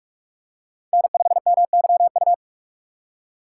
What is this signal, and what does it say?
Morse code, heard as a keyed tone, about 42 wpm, 700 Hz. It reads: N5MYU
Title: USA Call Signs - 5 Characters - 2x Character Spacing - 45wpm